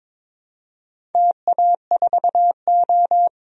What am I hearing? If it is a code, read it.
Morse code, heard as a keyed tone, about 22 wpm, 700 Hz. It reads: TA4O
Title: DXCC-100 Call Signs - 4 Characters - Repeated 3x - 22wpm